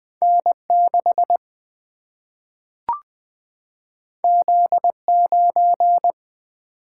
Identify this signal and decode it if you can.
Morse code, heard as a keyed tone, about 20 wpm, 700 Hz. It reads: N 6 Z 9
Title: Single-Letter Single-Number 20wpm